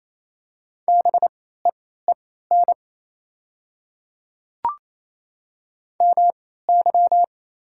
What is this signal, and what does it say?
Morse code, heard as a keyed tone, about 27 wpm, spaced as for 16 wpm, 700 Hz. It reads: BEEN MY
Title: 3x Character Spacing - Top 300 Words - 28wpm